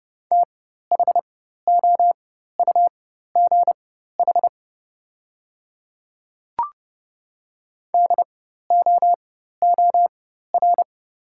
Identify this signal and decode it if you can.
Morse code, heard as a keyed tone, about 31 wpm, spaced as for 14 wpm, 700 Hz. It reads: THOUGH DOOR
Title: 4x Character Spacing - Top 500 Words - 30wpm